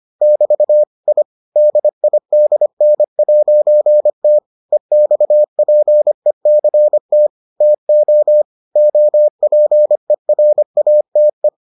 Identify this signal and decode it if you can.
Morse code, heard as a keyed tone, about 25 wpm, 600 Hz. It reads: <BT> I DIDN'T EXPECT TO OPERATE